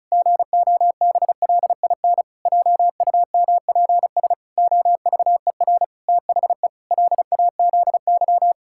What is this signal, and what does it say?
Morse code, heard as a keyed tone, about 36 wpm, 700 Hz. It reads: GOBLIN JUMPS OVER THE LAZY
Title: Pangram Challenge v2 - 35wpm